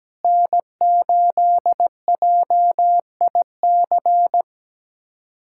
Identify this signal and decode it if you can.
Morse code, heard as a keyed tone, about 17 wpm, 700 Hz. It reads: N8JIC
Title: USA Callsign DE Callsign - 5 Characters - 17wpm